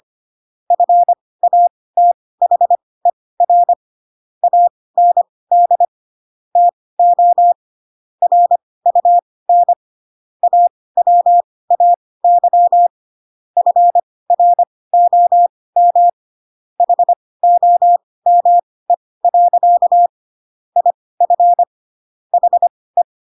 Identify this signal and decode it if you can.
Morse code, heard as a keyed tone, about 25 wpm, spaced as for 18 wpm, 700 Hz. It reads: FATHER AND TO RUN AWAY FROM HOME. IF HE